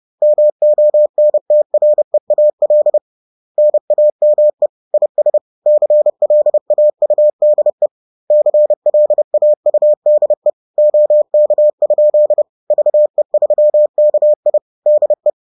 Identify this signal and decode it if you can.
Morse code, heard as a keyed tone, about 30 wpm, 600 Hz. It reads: MONTREAL NAME IS CLAUDE CLAUDE OK? VE3KI DE
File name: 30wpm-4